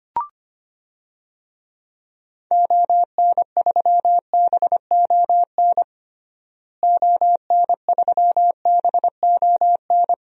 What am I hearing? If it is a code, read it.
Morse code, heard as a keyed tone, about 25 wpm, 700 Hz. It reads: ON3BON ON3BON
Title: DXCC-100 Call Signs - 6 Characters - Repeated 2x - 25wpm